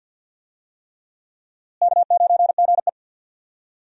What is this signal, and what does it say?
Morse code, heard as a keyed tone, about 46 wpm, 700 Hz. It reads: K9GE
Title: USA Call Signs - 4 Characters - Repeated 3x - 50wpm